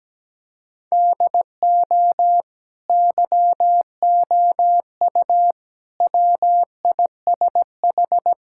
Morse code, heard as a keyed tone, about 17 wpm, 700 Hz. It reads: DO YOU WISH
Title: Sentences from Top 1600 Words 17wpm